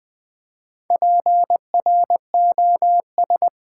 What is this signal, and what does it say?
Morse code, heard as a keyed tone, about 20 wpm, 700 Hz. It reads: PROS
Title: Build A Word - 4 Letters At A Time - 11-Letter Words - 20wpm